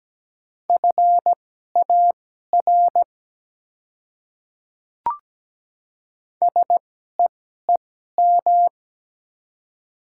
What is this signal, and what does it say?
Morse code, heard as a keyed tone, about 17 wpm, spaced as for 12 wpm, 700 Hz. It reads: FAR SEEM
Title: 2x Character Spacing - Top 300 Words - 17wpm